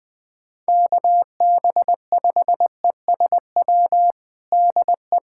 Morse code, heard as a keyed tone, about 20 wpm, 700 Hz. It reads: KB5ESW DE
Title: USA Callsign DE Callsign 20wpm